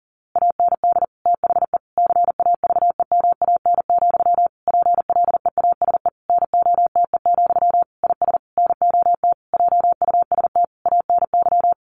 Morse code, heard as a keyed tone, about 40 wpm, 700 Hz. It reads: AND THE CAVEMAN, PLEASE NOTE, IS NOT JUST ANY
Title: A Sound Of Thunder 40wpm